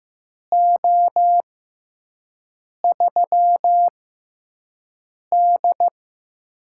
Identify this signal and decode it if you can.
Morse code, heard as a keyed tone, about 15 wpm, spaced as for 5 wpm, 700 Hz. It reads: O3D
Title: USA Call Signs - 6x Character Spacing - 15wpm